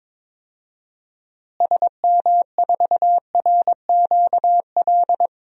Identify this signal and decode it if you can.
Morse code, heard as a keyed tone, about 22 wpm, 700 Hz. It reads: SM4RQL
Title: DXCC-100 Call Signs - 6 Characters - Repeated 2x - 22wpm